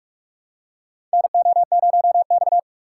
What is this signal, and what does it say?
Morse code, heard as a keyed tone, about 46 wpm, 700 Hz. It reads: NO0X
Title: USA Call Signs - 4 Characters - 3x Character Spacing - 45wpm